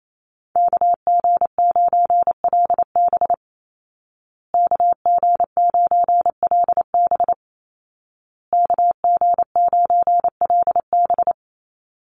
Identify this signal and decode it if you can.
Morse code, heard as a keyed tone, about 28 wpm, 700 Hz. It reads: KG9LB KG9LB KG9LB
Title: Call Signs Repeated 2x - 28wpm